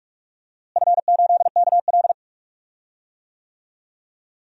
Morse code, heard as a keyed tone, about 46 wpm, 700 Hz. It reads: F8KL
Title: DXCC-100 Call Signs - 4 Characters - Repeated 3x - 45wpm